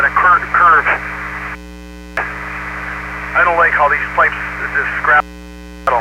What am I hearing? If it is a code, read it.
Emergency Alert System (SAME) tones (header bursts), heard as harsh buzzing bursts, with speech. It is heard from 0.8 to 5.2 s.